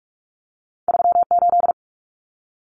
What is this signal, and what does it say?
Morse code, heard as a keyed tone, about 46 wpm, 700 Hz. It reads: 38
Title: Numbers - 2-Digit 45wpm